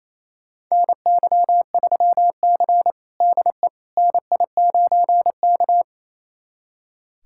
Morse code, heard as a keyed tone, about 28 wpm, 700 Hz. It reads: NY3C DE NI9K
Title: USA Callsign DE Callsign - 4 Characters - 28wpm